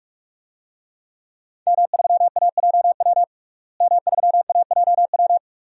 Morse code, heard as a keyed tone, about 45 wpm, 700 Hz. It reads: M3AJW M3AJW
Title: DXCC-100 Call Signs - 5 Characters - Repeated 2x - 45wpm